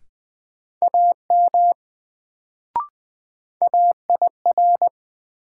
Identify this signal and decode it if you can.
Morse code, heard as a keyed tone, about 20 wpm, 700 Hz. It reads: AM AIR
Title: Sets of 2 Words - New Word Drill - 20wpm